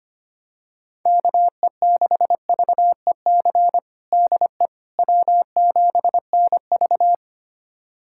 Morse code, heard as a keyed tone, about 25 wpm, 700 Hz. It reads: KE6VEC DE W7NV
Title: USA Callsign DE Callsign 25wpm